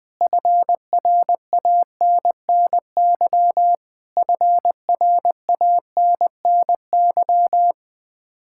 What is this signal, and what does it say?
Morse code, heard as a keyed tone, about 20 wpm, 700 Hz. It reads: FRANNY FRANNY
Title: Practice QSO - Exchange 1 - Beginning - 20wpm